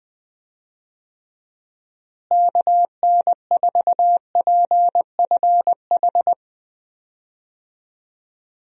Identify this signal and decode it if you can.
Morse code, heard as a keyed tone, about 20 wpm, 700 Hz. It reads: KN4PFH
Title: USA Call Signs - 6 Characters - Repeated 3x - 20wpm